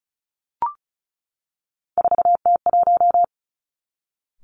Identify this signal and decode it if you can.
Morse code, heard as a keyed tone, about 34 wpm, 700 Hz. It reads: 4T1
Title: Three Challenging Letter-Number Combinations 35wpm